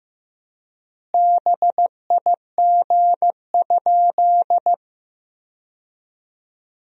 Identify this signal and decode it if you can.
Morse code, heard as a keyed tone, about 15 wpm, 700 Hz. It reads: BIG?
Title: 4x Word-Spacing - Sentences from Top 200 Words - 15wpm